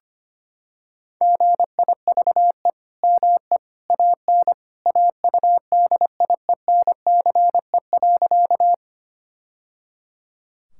Morse code, heard as a keyed tone, about 25 wpm, 700 Hz. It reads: GIVE ME AN AUDIENCE.
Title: Sentences from Top 1000 Words - Review Part 5 - 25wpm